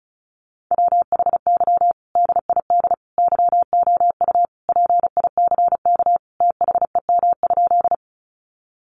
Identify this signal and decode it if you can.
Morse code, heard as a keyed tone, about 34 wpm, 700 Hz. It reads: WHY DID YOU PICK THEM?
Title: Sentences from Top 1000 Words 35wpm